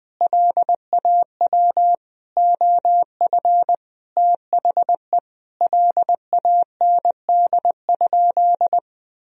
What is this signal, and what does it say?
Morse code, heard as a keyed tone, about 21 wpm, 700 Hz. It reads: LAW OF THE LAND?
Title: Sentences from Top 1000 Words - Review Part 3 - 20wpm